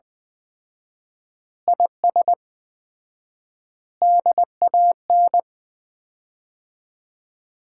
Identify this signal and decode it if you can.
Morse code, heard as a keyed tone, about 20 wpm, 700 Hz. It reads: IS DAN
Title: Common QSO Phrases - 4x Word-Spacing - 20wpm